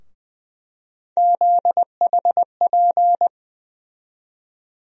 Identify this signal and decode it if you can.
Morse code, heard as a keyed tone, about 21 wpm, 700 Hz. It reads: ZHP
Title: USA Call Sign Postfixes - 3 Letters 20wpm